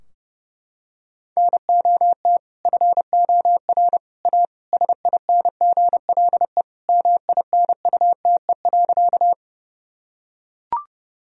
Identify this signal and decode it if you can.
Morse code, heard as a keyed tone, about 30 wpm, 700 Hz. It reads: NOT FOR A SINGLE MINUTE.
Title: Sentences from Top 1000 Words - Review Part 4 - 30wpm